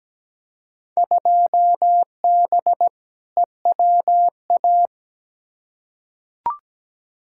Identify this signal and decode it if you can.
Morse code, heard as a keyed tone, about 17 wpm, 700 Hz. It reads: 2B EWA
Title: Field Day Exchanges Encore - 17wpm